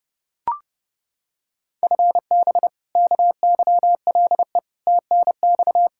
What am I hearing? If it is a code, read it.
Morse code, heard as a keyed tone, about 32 wpm, 700 Hz. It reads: FB KYLE TNX
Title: Practice QSO - Exchange 3 - More Info - 30wpm